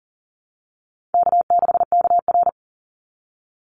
Morse code, heard as a keyed tone, about 37 wpm, 700 Hz. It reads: K6KR
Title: Easy Call Signs Repeated 3x - 40wpm